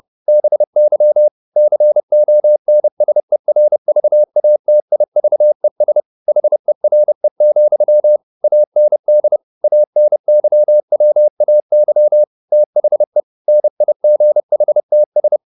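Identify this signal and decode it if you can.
Morse code, heard as a keyed tone, about 30 wpm, 600 Hz. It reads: BY CONSERVATIVES HERE, AND ANYWAY THE NIGHTS
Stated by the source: Morse code at 30 wpm, the novel Never by Ken Follett in English, from CW med Gnister.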